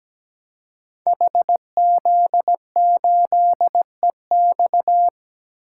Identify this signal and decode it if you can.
Morse code, heard as a keyed tone, about 17 wpm, 700 Hz. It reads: HZ8EX
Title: Code Groups - 5 Letter:Numbers - 17wpm